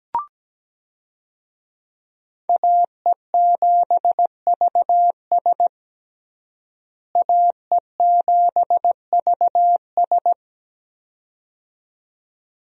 Morse code, heard as a keyed tone, about 17 wpm, 700 Hz. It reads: AE7VS AE7VS
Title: USA Call Signs - 5 Characters - Repeated 2x - 17wpm